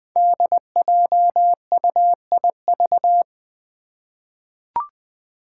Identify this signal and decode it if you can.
Morse code, heard as a keyed tone, about 20 wpm, 700 Hz. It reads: DJUIV
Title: Five Challenging Letter-Number Combinations 20wpm